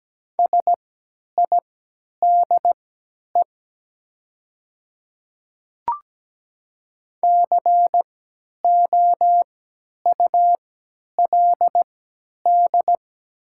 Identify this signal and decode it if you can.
Morse code, heard as a keyed tone, about 17 wpm, spaced as for 10 wpm, 700 Hz. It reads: SIDE COULD
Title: 3x Character Spacing - Top 100 Words - 17wpm